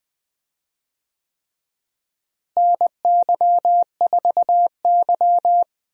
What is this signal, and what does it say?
Morse code, heard as a keyed tone, about 21 wpm, 700 Hz. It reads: NY4Y
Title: USA Call Signs - 4 Characters - Repeated 3x - 20wpm